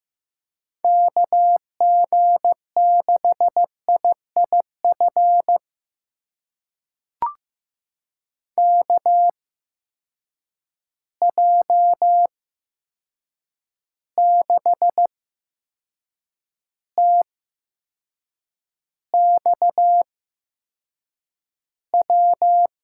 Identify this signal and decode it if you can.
Morse code, heard as a keyed tone, about 15 wpm, spaced as for 6 wpm, 700 Hz. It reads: KG6IIF KJ6TXW
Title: USA Call Signs - 6 Characters - 8x Character Spacing - 15wpm